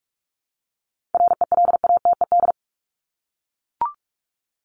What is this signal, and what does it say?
Morse code, heard as a keyed tone, about 44 wpm, 700 Hz. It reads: RELATED
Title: ED Suffix - Top 100 Words with Base Words 45wpm